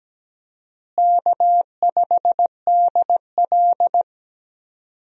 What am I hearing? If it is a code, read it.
Morse code, heard as a keyed tone, about 17 wpm, 700 Hz. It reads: K5DL
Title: USA Call Signs - 4 Characters - Repeated 3x - 17wpm